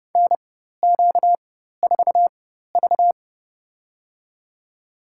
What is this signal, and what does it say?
Morse code, heard as a keyed tone, about 30 wpm, spaced as for 14 wpm, 700 Hz. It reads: NQ4V
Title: USA Call Signs - 4 Characters - 4x Character Spacing - 30wpm